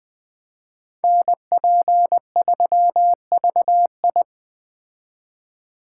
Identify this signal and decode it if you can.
Morse code, heard as a keyed tone, about 20 wpm, 700 Hz. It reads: NP3VI
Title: Speed Racing - USA 2x2 Call Signs 17wpm